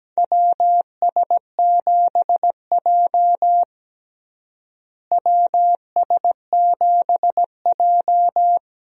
Morse code, heard as a keyed tone, about 17 wpm, 700 Hz. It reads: WS7J WS7J
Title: USA Call Signs - 4 Characters - Repeated 3x - 17wpm